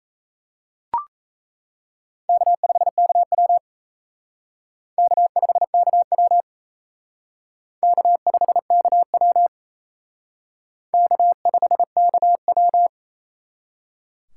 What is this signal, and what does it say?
Morse code, heard as a keyed tone, about 37 wpm, 700 Hz. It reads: K5KW K5KW K5KW K5KW